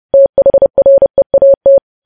Morse code, heard as a keyed tone, about 30 wpm, 550 Hz. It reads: THREAT
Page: News Headlines in Morse Code at 30 WPM